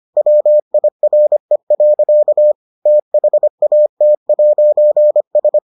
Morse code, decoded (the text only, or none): WIRE. THAT'S